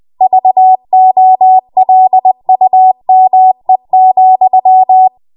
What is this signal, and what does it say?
Morse code, heard as a keyed tone, about 20 wpm, 750 Hz. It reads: VOLUME,
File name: Morse / 20WPM